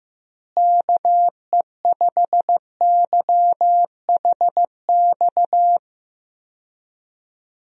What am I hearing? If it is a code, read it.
Morse code, heard as a keyed tone, about 16 wpm, 700 Hz. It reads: KE5YHX
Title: USA Call Signs - 6 Characters - Repeated 3x - 15wpm